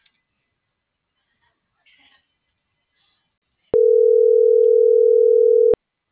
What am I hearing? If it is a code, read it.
Ringback tone (North America), 440 + 480 Hz.